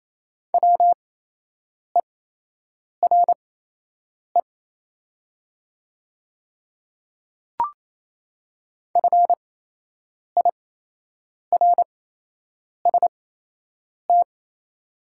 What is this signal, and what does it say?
Morse code, heard as a keyed tone, about 28 wpm, spaced as for 8 wpm, 700 Hz. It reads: WERE FIRST
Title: 8x Character Spacing - Top 100 Words - 28wpm